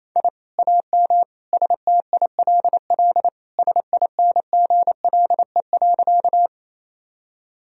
Morse code, heard as a keyed tone, about 28 wpm, 700 Hz. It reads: I AM STILL SINGLE.
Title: Sentences from Top 1000 Words - Review Part 4 - 28wpm